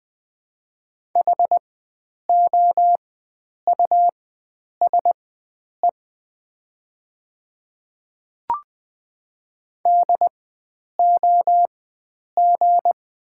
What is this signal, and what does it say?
Morse code, heard as a keyed tone, about 20 wpm, spaced as for 9 wpm, 700 Hz. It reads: HOUSE DOG
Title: 4x Character Spacing - Top 500 Words - 20wpm